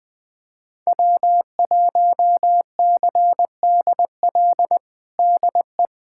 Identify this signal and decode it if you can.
Morse code, heard as a keyed tone, about 20 wpm, 700 Hz. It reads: W1CDL DE
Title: USA Callsign DE Callsign - 5 Characters - 20wpm